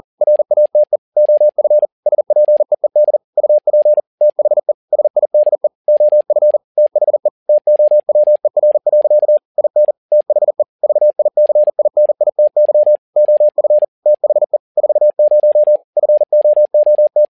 Morse code, heard as a keyed tone, about 40 wpm, 600 Hz. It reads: RATE OF SPEED UP THE SIDE OF THE TOWER. IN THE VICINITY OF THE 40 FOOT
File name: learning center / tower trouble / tower touble 40wpm